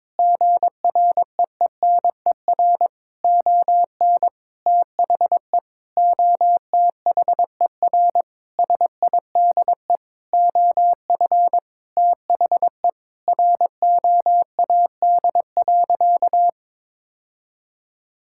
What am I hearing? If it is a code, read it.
Morse code, heard as a keyed tone, about 22 wpm, 700 Hz. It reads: GREENER ON THE OTHER SIDE OF THE ROAD.
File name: practice / 22wpm / Sentences with 4-Word Phrases - 22wpm